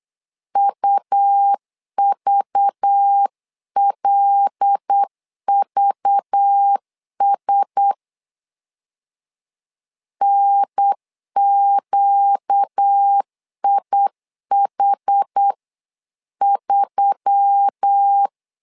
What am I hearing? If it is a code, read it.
Morse code, heard as a keyed tone, about 8 wpm, 800 Hz. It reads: UVLVS NQIH3
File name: LEZIONI / 5 LBVC / 5 LBV 8wpm 2837